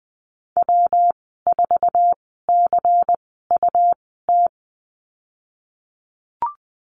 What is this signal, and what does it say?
Morse code, heard as a keyed tone, about 20 wpm, spaced as for 15 wpm, 700 Hz. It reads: W4CUT
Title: 122 Lesson 20g - Number 3 - Call Signs 20wpm